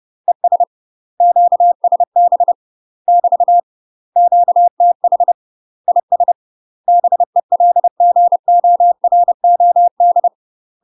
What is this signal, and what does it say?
Morse code, heard as a keyed tone, about 30 wpm, 700 Hz. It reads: ES QSB <BT> QTH IS BELGOROD
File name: UK MASTER QSOs TRAINING / 30WPM / UK MASTER QSOs 0039